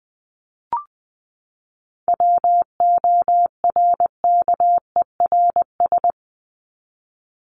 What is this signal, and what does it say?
Morse code, heard as a keyed tone, about 20 wpm, 700 Hz. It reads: WORKERS
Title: ES:S Suffix - Top 100 Words with Base Words 20wpm